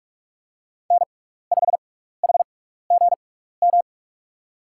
Morse code, heard as a keyed tone, about 44 wpm, spaced as for 15 wpm, 690 Hz. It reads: N5HGM
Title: USA Call Signs - 5 Characters - 6x Character Spacing - 45wpm